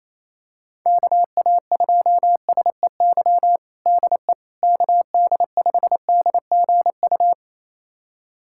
Morse code, heard as a keyed tone, about 28 wpm, 700 Hz. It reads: KA2SEY DE KD5DGU
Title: USA Callsign DE Callsign 28wpm